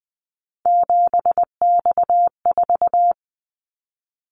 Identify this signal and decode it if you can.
Morse code, heard as a keyed tone, about 20 wpm, 700 Hz. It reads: 7X4
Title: Callsign Prefixes for Top 100 DXCC - 20wpm